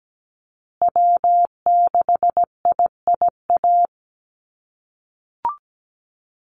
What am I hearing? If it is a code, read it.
Morse code, heard as a keyed tone, about 17 wpm, 700 Hz. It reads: W6IIA
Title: USA Call Signs Repeated 2x 17wpm